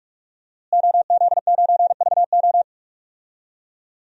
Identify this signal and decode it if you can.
Morse code, heard as a keyed tone, about 46 wpm, 700 Hz. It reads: OZ9VO
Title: DXCC-100 Call Signs - 5 Characters - Repeated 2x - 45wpm